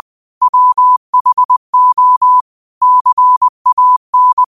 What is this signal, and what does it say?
Morse code, heard as a keyed tone, about 20 wpm, 1000 Hz. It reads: WHO CAN
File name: qotdfiles / 20wpm